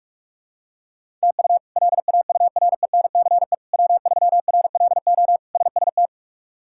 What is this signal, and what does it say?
Morse code, heard as a keyed tone, about 46 wpm, 700 Hz. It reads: TU LAURENCE W3RLO SST
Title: K1USN SST QSO Practice - 45wpm